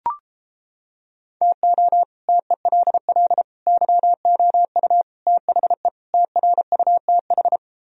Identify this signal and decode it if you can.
Morse code, heard as a keyed tone, about 29 wpm, 700 Hz. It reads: TO TELL YOU THE TRUTH